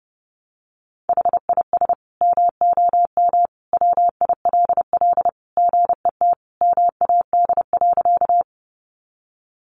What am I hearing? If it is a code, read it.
Morse code, heard as a keyed tone, about 28 wpm, 700 Hz. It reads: HIS MOM WILL GET MAD.
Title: Additional Easy Phrases 30wpm